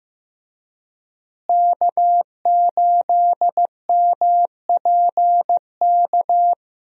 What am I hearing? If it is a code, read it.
Morse code, heard as a keyed tone, about 15 wpm, 700 Hz. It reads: K8MPK
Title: USA Call Signs - 5 Characters - Repeated 2x - 15wpm